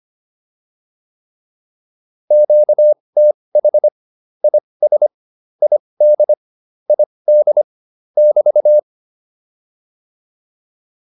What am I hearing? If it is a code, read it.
Morse code, heard as a keyed tone, about 25 wpm, spaced as for 20 wpm, 600 Hz. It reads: QTH IS ID ID <BT>
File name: QSO105a 20